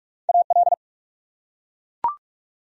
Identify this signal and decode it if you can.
Morse code, heard as a keyed tone, about 42 wpm, 700 Hz. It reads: AL
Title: USA Common Names - Max 4 Letters - 45wpm